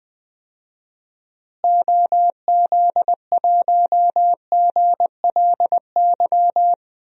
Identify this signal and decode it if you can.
Morse code, heard as a keyed tone, about 21 wpm, 700 Hz. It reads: OZ1GLY
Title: DXCC-100 Call Signs - 6 Characters - Repeated 2x - 20wpm